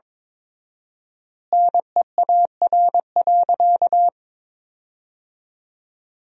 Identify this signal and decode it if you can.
Morse code, heard as a keyed tone, about 21 wpm, 700 Hz. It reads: NEAR.
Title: 4x Word-Spacing - Sentences from Top 300 Words - 22wpm